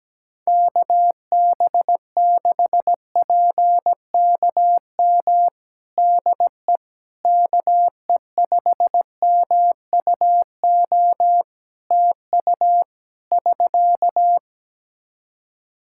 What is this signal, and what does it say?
Morse code, heard as a keyed tone, about 17 wpm, 700 Hz. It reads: KB6PKM DE KE5MUO TU <SK>
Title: Practice QSO - Exchange 4 - Ending - 17wpm